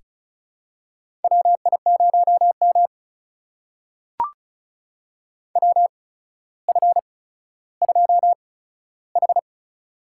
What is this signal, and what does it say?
Morse code, heard as a keyed tone, about 33 wpm, spaced as for 15 wpm, 700 Hz. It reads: WI0M WF2H
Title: USA Call Signs - 4 Characters - 8x Character Spacing - 35wpm